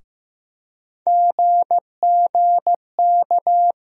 Morse code, heard as a keyed tone, about 15 wpm, 700 Hz. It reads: GGK
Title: USA Call Sign Postfixes - 3 Letters 15wpm